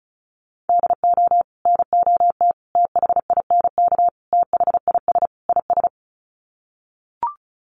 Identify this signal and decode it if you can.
Morse code, heard as a keyed tone, about 34 wpm, 700 Hz. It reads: DO NOT THINK THIS IS
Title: Sets of 5 Words Encore2 35wpm